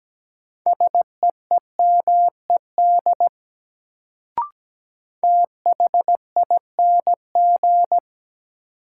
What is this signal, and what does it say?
Morse code, heard as a keyed tone, about 17 wpm, 700 Hz. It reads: SEEMED THING
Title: Sets of 3 Words - New Word Drill - 17wpm